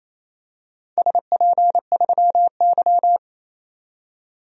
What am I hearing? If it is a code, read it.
Morse code, heard as a keyed tone, about 27 wpm, 700 Hz. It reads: SP3Y
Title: DXCC-100 Call Signs - 4 Characters - 28wpm